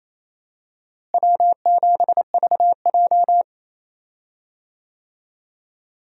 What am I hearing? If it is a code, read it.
Morse code, heard as a keyed tone, about 28 wpm, 700 Hz. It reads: W7VJ
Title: USA Call Signs - 4 Characters - Repeated 2x - 28wpm